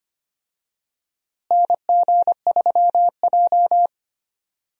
Morse code, heard as a keyed tone, about 25 wpm, 700 Hz. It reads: NG3J
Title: USA Call Signs - 4 Characters - Repeated 2x - 25wpm